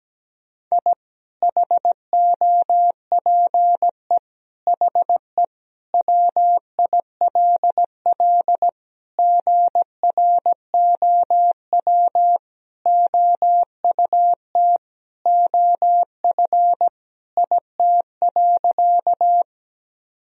Morse code, heard as a keyed tone, about 17 wpm, 700 Hz. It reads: I HOPE HE WILL GROW OUT OF IT.
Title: Sentences from Top 1000 Words - Review Part 1 - 17wpm